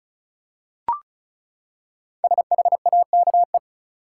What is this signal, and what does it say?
Morse code, heard as a keyed tone, about 34 wpm, 700 Hz. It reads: SHAKE
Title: Five Letter Words - 35wpm